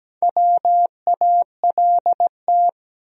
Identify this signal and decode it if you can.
Morse code, heard as a keyed tone, about 17 wpm, 700 Hz. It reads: WALT
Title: USA Common Names - Max 4 Letters - 17wpm